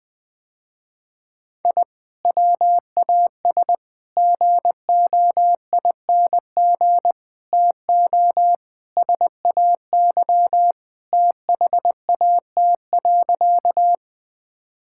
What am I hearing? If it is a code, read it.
Morse code, heard as a keyed tone, about 20 wpm, 700 Hz. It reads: I WAS GOING TO SAY THAT.
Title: Sentences with 5-Word Phrases - Spoken Beginning - 20wpm